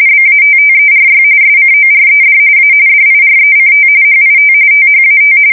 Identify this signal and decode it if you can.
RTTY, about 45.45 baud. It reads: BATTERY SCORED A DIRECT HIT THAT
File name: fsk170 / USS-New-Jersey